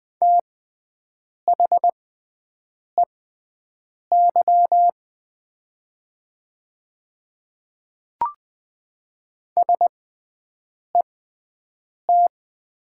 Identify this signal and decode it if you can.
Morse code, heard as a keyed tone, about 20 wpm, spaced as for 7 wpm, 700 Hz. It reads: THEY SET
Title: 6x Character Spacing - Top 300 Words - 20wpm